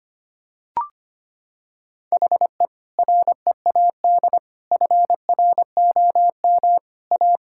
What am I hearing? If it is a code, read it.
Morse code, heard as a keyed tone, about 25 wpm, 700 Hz. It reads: HE READ FROM A
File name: practice / 25wpm / Sentences with 2-Word Phrases - 25wpm